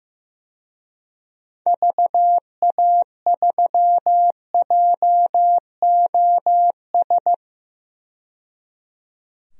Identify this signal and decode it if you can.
Morse code, heard as a keyed tone, about 15 wpm, 700 Hz. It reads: VA3JOS
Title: DXCC-100 Call Signs - 6 Characters - 15wpm